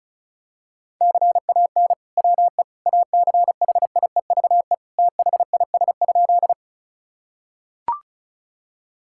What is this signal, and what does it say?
Morse code, heard as a keyed tone, about 36 wpm, 690 Hz. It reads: CAN WE ACHIEVE THIS?
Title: Sentences from Top 1000 Words - Review Part 6 - 35wpm